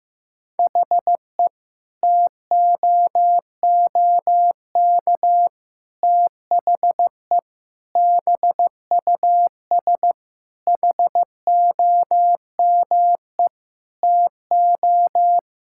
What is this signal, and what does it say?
Morse code, heard as a keyed tone, about 15 wpm, 700 Hz. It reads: HE TOOK THE BUS HOME TO
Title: Sentences from Top 1000 Words - Review Part 6 - 15wpm